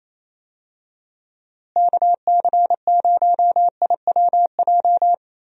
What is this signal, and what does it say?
Morse code, heard as a keyed tone, about 28 wpm, 700 Hz. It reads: KC0IWJ
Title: USA Call Signs - 6 Characters - Repeated 3x - 28wpm